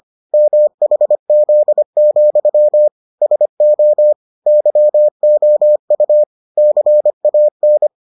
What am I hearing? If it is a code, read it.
Morse code, heard as a keyed tone, about 25 wpm, 600 Hz. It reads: MHZ, SO YOU CAN